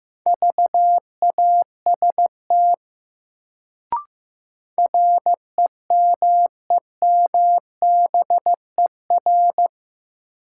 Morse code, heard as a keyed tone, about 15 wpm, 700 Hz. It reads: VAST REMEMBER
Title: Sets of 4 Words - New Word Drill - 15wpm